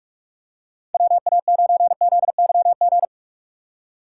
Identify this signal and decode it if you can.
Morse code, heard as a keyed tone, about 45 wpm, 700 Hz. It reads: WA9ZYG
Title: USA Call Signs - 6 Characters - 4x Character Spacing - 45wpm